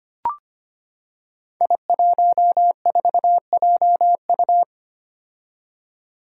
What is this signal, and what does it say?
Morse code, heard as a keyed tone, about 25 wpm, 700 Hz. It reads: I14JU
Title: Five Challenging Letter-Number Combinations 25wpm